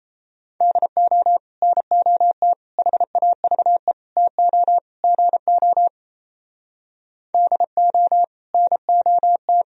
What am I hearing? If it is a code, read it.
Morse code, heard as a keyed tone, about 32 wpm, 700 Hz. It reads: DO NOT HAVE TO GO DO NOT
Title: Speed-Racing - Sets of 5 Words 28wpm